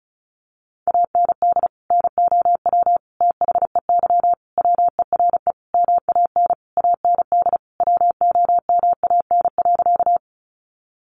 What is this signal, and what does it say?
Morse code, heard as a keyed tone, about 34 wpm, 700 Hz. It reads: AND NOW THEY WERE MAN AND WOMAN.
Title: Sentences from Top 1100 Words 35wpm